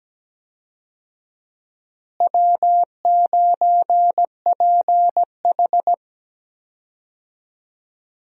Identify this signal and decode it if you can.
Morse code, heard as a keyed tone, about 17 wpm, 700 Hz. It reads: W9PH
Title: USA Call Signs - 4 Characters - Repeated 3x - 17wpm